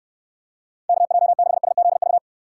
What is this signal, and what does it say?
Morse code, heard as a keyed tone, about 69 wpm, 700 Hz. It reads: KJ6SZV